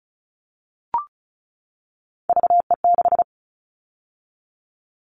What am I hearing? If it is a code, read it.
Morse code, heard as a keyed tone, about 34 wpm, 700 Hz. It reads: VE6
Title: All Cut Numbers - 3-Digit 35wpm